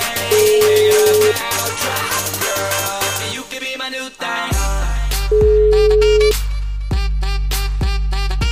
Ringback tone (Europe, 425 Hz), with noise.